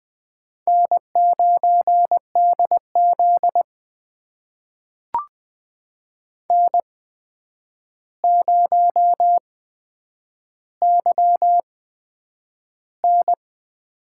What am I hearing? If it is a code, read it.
Morse code, heard as a keyed tone, about 20 wpm, spaced as for 9 wpm, 700 Hz. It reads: N9DZ N0YN
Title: USA Call Signs - 4 Characters - 8x Character Spacing - 20wpm